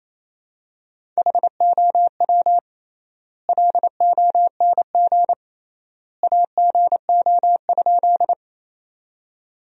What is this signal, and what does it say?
Morse code, heard as a keyed tone, about 28 wpm, 700 Hz. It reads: HOW LONG AGO?
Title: 3x Word-Spacing - Sentences from Top 500 Words - 28wpm